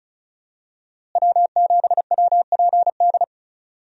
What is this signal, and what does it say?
Morse code, heard as a keyed tone, about 36 wpm, 700 Hz. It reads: W7WPD
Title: USA Call Signs - 5 Characters - Repeated 2x - 35wpm